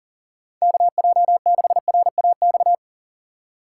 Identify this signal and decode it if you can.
Morse code, heard as a keyed tone, about 37 wpm, 700 Hz. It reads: KJ6RAX
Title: USA Call Signs - 2x Character Spacing - 40wpm